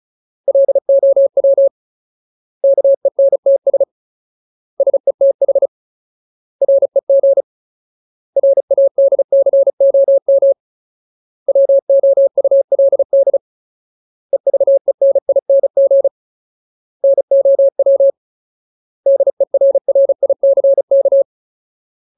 Morse code, heard as a keyed tone, about 34 wpm, 550 Hz. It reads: LOW KENTS SETH REG RADCOM WOULD EVENING NOW DERRICK
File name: Ultimate Word List / 35WPM Ultimate